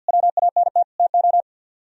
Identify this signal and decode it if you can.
Morse code, heard as a keyed tone, about 57 wpm, 690 Hz. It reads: WANT TO
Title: Speed-Racing - Sets of 5 Words 50wpm